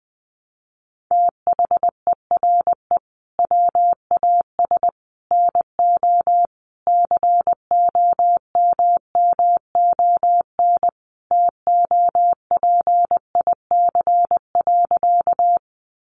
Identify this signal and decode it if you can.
Morse code, heard as a keyed tone, about 20 wpm, 700 Hz. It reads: THERE WAS NO COMMON TOPIC.
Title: Sentences from Top 1200 Words 20wpm